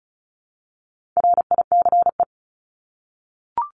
Morse code, heard as a keyed tone, about 34 wpm, 700 Hz. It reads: RICE
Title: Top 1501-1750 Words 35wpm